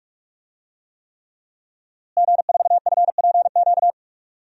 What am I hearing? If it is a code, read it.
Morse code, heard as a keyed tone, about 43 wpm, 700 Hz. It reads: G4FPQ
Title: DXCC-100 Call Signs - 5 Characters - Repeated 3x - 45wpm